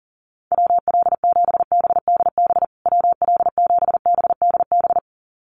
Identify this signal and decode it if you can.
Morse code, heard as a keyed tone, about 40 wpm, 700 Hz. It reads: WL7BDB WL7BDB
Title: USA Call Signs Repeated 2x 40wpm